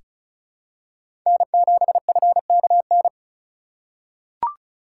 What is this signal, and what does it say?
Morse code, heard as a keyed tone, about 34 wpm, 700 Hz. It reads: N7FKN
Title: USA Call Signs - 5 Characters - 3x Character Spacing - 35wpm